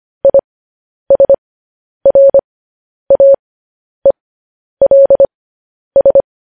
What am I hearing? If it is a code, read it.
Morse code, heard as a keyed tone, about 25 wpm, spaced as for 10 wpm, 550 Hz. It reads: ISRAELS